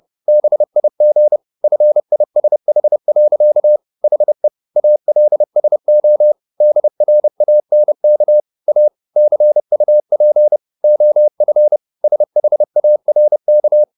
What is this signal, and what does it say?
Morse code, heard as a keyed tone, about 30 wpm, 600 Hz. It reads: BIG FISH. HE ALSO DRANK A CUP OF SHARK
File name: the old man and the sea / the old man and the sea 1 30wpm